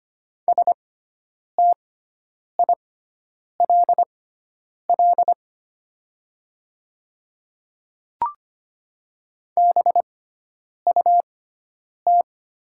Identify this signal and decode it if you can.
Morse code, heard as a keyed tone, about 25 wpm, spaced as for 9 wpm, 700 Hz. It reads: STILL BUT
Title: 6x Character Spacing - Top 300 Words - 25wpm